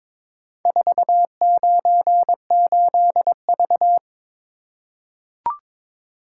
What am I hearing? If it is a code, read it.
Morse code, heard as a keyed tone, about 22 wpm, 700 Hz. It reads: 498V
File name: practice / 22wpm / Four Letter-Number Combinations - 22wpm